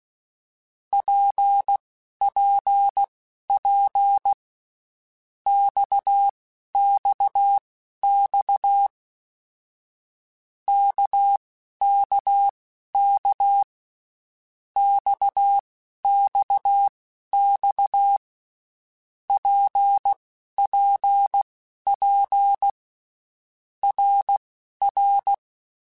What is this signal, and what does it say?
Morse code, heard as a keyed tone, about 16 wpm, spaced as for 12 wpm, 800 Hz. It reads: PPP XXX KKK XXX PPP RR
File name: Basic Letter Groups By7Groups / RKPX / RKPX@10